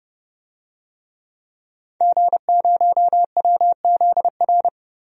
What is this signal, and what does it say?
Morse code, heard as a keyed tone, about 31 wpm, 700 Hz. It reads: G0WZR
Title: DXCC-100 Call Signs - 5 Characters - 30wpm